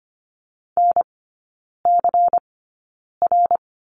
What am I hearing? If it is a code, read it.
Morse code, heard as a keyed tone, about 25 wpm, spaced as for 9 wpm, 700 Hz. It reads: NCR
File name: Lesson 17e - Letter u - Three Characters 25wpm